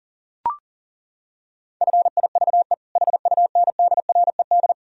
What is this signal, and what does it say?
Morse code, heard as a keyed tone, about 40 wpm, 700 Hz. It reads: FIVE HUNDRED
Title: Sentences from Top 1000 Words - Review Part 3 - 40wpm